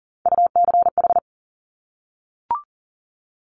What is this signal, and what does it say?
Morse code, heard as a keyed tone, about 40 wpm, 700 Hz. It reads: UCH
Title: Top Three Letter Combinations in Words 40wpm